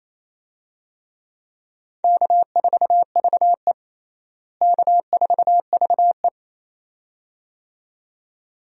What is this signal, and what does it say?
Morse code, heard as a keyed tone, about 28 wpm, 700 Hz. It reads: K4VE K4VE
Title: USA Call Signs - 4 Characters - Repeated 2x - 28wpm